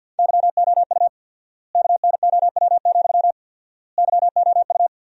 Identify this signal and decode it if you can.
Morse code, heard as a keyed tone, about 50 wpm, 700 Hz. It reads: YOU KNOW, YOU
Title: 4x Word-Spacing - Sentences from Top 200 Words - 50wpm